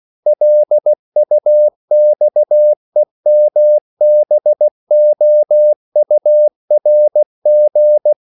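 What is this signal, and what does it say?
Morse code, heard as a keyed tone, about 16 wpm, 600 Hz. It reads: LUXEMBOURG
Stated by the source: CW practice of national capitals at 16 words per minute, from ON6ZQ.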